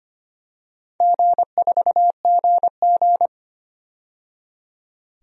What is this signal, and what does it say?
Morse code, heard as a keyed tone, about 25 wpm, 700 Hz. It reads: G4GG
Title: DXCC-100 Call Signs - 4 Characters - 25wpm